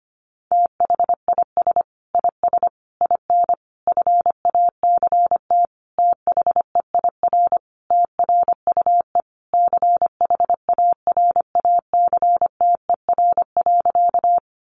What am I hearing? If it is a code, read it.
Morse code, heard as a keyed tone, about 25 wpm, 700 Hz. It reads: THIS IS IN FACT THEIR TRUE CHARACTER.
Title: Sentences from Top 1200 Words 25wpm